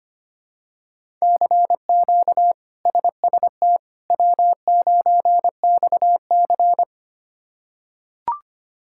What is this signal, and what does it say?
Morse code, heard as a keyed tone, about 25 wpm, 700 Hz. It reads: CQ SST W9XC
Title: K1USN SST QSO Practice - 25wpm